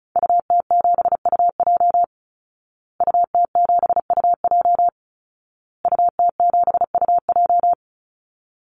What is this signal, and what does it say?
Morse code, heard as a keyed tone, about 34 wpm, 700 Hz. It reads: UT7UJ UT7UJ UT7UJ
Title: Call Signs Repeated 2x - 35wpm